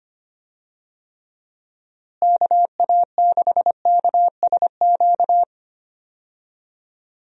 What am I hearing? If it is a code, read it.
Morse code, heard as a keyed tone, about 25 wpm, 700 Hz. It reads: KA6KSQ
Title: USA Call Signs - 6 Characters - Repeated 3x - 25wpm